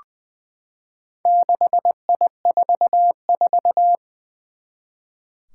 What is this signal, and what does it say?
Morse code, heard as a keyed tone, about 20 wpm, 700 Hz. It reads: 6I44
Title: Four Challenging Letter-Number Combinations 20wpm